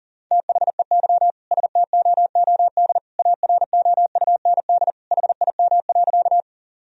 Morse code, heard as a keyed tone, about 40 wpm, 700 Hz. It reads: THEY STOOD AROUND HIM.
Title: Sentences from Top 1000 Words - Review Part 3 - 40wpm